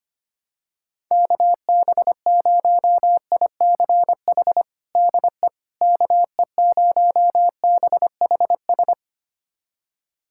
Morse code, heard as a keyed tone, about 25 wpm, 700 Hz. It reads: KB0ICH DE KE0BHS
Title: USA Callsign DE Callsign - 6 Characters - 25wpm